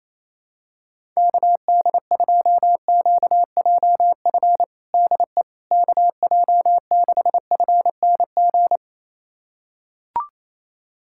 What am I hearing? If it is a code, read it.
Morse code, heard as a keyed tone, about 27 wpm, 700 Hz. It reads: KD2QJF DE KJ6FNG
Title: USA Callsign DE Callsign - 6 Characters - 28wpm